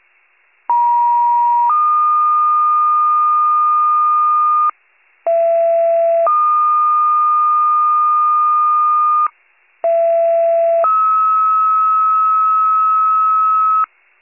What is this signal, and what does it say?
Two-tone page, heard as a one-second tone, then a three-second tone, about 950 then 1220 Hz, about 670 then 1220 Hz, about 670 then 1290 Hz. It is heard from 0.7 to 13.8 s.